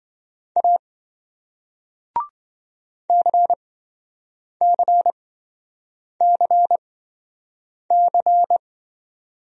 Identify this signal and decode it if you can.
Morse code, heard as a keyed tone, about 27 wpm, spaced as for 7 wpm, 700 Hz. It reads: A CCCC